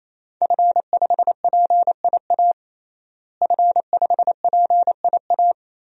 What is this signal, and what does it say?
Morse code, heard as a keyed tone, about 28 wpm, 700 Hz. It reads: F5PIA F5PIA
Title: DXCC-100 Call Signs - 5 Characters - Repeated 3x - 28wpm